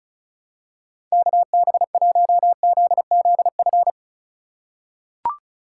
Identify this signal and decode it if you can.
Morse code, heard as a keyed tone, about 36 wpm, 700 Hz. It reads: KB1ZZF
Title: USA Call Signs - 6 Characters - 3x Character Spacing - 35wpm